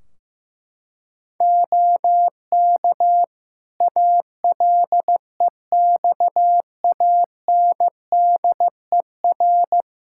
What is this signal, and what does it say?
Morse code, heard as a keyed tone, about 16 wpm, 700 Hz. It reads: OK ALEXANDER
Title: K1USN SST QSO Practice 15wpm